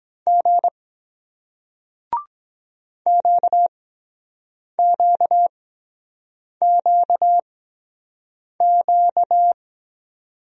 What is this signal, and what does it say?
Morse code, heard as a keyed tone, about 23 wpm, spaced as for 7 wpm, 700 Hz. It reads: G QQQQ